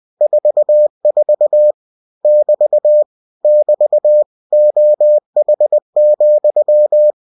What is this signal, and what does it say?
Morse code, heard as a keyed tone, about 20 wpm, 600 Hz. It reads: 44 <BT> <BT> OH,
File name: cw-books / winnie-the-pooh / winnie-20wpm-page-0044